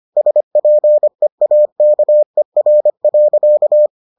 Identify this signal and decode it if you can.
Morse code, heard as a keyed tone, about 25 wpm, 600 Hz. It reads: SPEAKER.